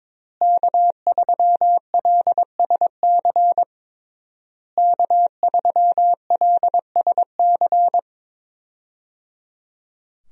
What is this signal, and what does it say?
Morse code, heard as a keyed tone, about 21 wpm, 700 Hz. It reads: K3LSC K3LSC
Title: USA Call Signs - 5 Characters - Repeated 3x - 22wpm